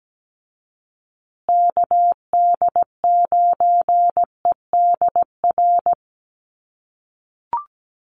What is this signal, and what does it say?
Morse code, heard as a keyed tone, about 17 wpm, 700 Hz. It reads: KD9EDR
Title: USA Call Signs Repeated 2x 17wpm